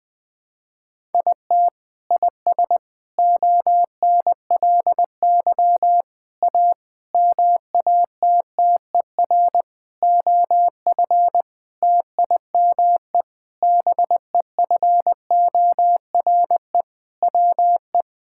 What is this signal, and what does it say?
Morse code, heard as a keyed tone, about 20 wpm, 700 Hz. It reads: IT IS ONLY A MATTER OF TIME BEFORE WE
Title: Sentences with 8-Word Phrases - 20wpm